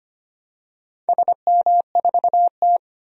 Morse code, heard as a keyed tone, about 25 wpm, 700 Hz. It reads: SM4T
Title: DXCC-100 Call Signs - 4 Characters - 25wpm